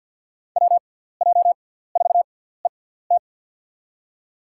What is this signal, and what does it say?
Morse code, heard as a keyed tone, about 49 wpm, spaced as for 17 wpm, 700 Hz. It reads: WJ4ET
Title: USA Call Signs - 5 Characters - 6x Character Spacing - 50wpm